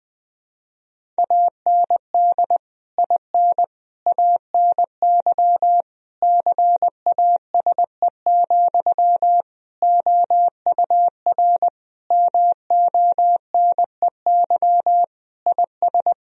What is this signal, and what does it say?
Morse code, heard as a keyed tone, about 20 wpm, 700 Hz. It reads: AND IN ANY CASE, OUR MONEY IS
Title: Sentences from Top 1000 Words - Review Part 1 - 20wpm